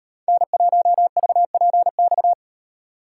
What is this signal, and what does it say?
Morse code, heard as a keyed tone, about 38 wpm, 700 Hz. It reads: N1VPX